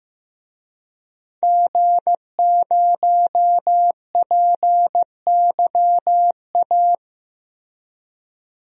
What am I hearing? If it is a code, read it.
Morse code, heard as a keyed tone, about 15 wpm, 700 Hz. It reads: G0PYA
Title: DXCC-100 Call Signs - 5 Characters - Repeated 3x - 15wpm